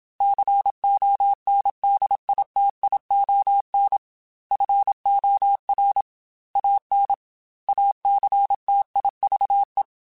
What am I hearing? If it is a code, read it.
Morse code, heard as a keyed tone, about 27 wpm, 800 Hz. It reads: CONDITION FOR AN ACTIVE
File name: Technical Subjects / Technical Subjects-mp3s / AND Gate @25